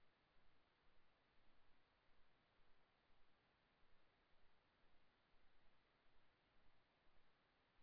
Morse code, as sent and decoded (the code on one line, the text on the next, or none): none